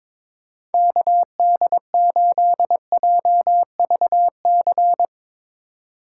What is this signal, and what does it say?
Morse code, heard as a keyed tone, about 22 wpm, 700 Hz. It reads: KD8JVC
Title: Speed Racing - USA Call Signs 22wpm